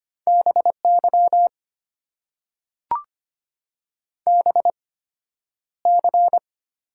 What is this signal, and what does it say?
Morse code, heard as a keyed tone, about 25 wpm, spaced as for 11 wpm, 690 Hz. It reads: BY BC
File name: Top 90 QSO Elements - 8x Character Spacing - 25wpm